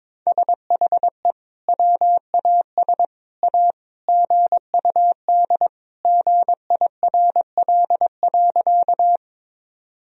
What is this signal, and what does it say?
Morse code, heard as a keyed tone, about 22 wpm, 700 Hz. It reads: SHE WAS A GUD GIRL.
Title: Sentences from Top 400 Words and Abbreviations 22wpm